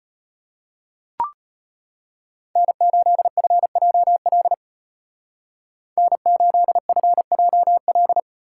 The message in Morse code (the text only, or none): N8FJL N8FJL